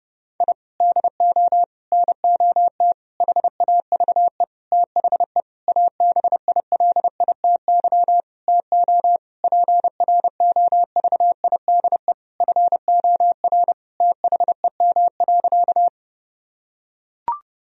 Morse code, heard as a keyed tone, about 30 wpm, 700 Hz. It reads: I DO NOT HAVE THE ABILITY TO PROVIDE FOR THEM.
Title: Sentences from Top 1000 Words - Review Part 3 - 30wpm